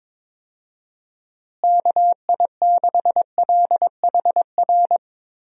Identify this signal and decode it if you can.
Morse code, heard as a keyed tone, about 22 wpm, 700 Hz. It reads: KI6LHR
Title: USA Call Signs - 6 Characters - Repeated 3x - 22wpm